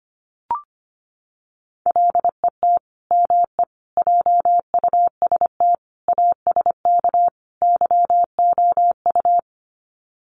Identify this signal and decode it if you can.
Morse code, heard as a keyed tone, about 25 wpm, 700 Hz. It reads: LET ME JUST ASK YOU
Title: Sets of 5 Words Encore2 25wpm